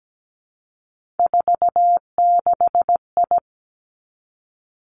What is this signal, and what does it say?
Morse code, heard as a keyed tone, about 17 wpm, 700 Hz. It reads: 46I
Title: Three Challenging Letter-Number Combinations 17wpm